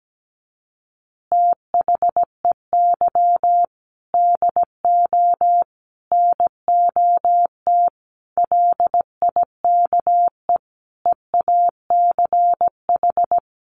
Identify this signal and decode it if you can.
Morse code, heard as a keyed tone, about 17 wpm, 700 Hz. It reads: THEY DO NOT LIKE EACH
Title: Easy Phrases 17wpm